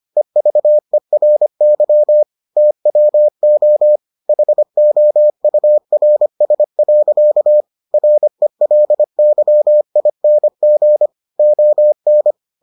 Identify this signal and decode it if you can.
Morse code, heard as a keyed tone, about 25 wpm, 600 Hz. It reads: EVERY TWO HOURS. RELYING ON